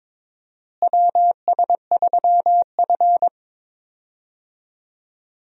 Morse code, heard as a keyed tone, about 22 wpm, 700 Hz. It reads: WS3F
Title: USA Call Signs - 4 Characters - Repeated 2x - 22wpm